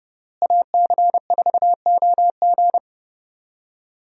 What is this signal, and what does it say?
Morse code, heard as a keyed tone, about 30 wpm, 700 Hz. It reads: AC4OG
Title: USA Call Signs - 5 Characters - Repeated 3x - 30wpm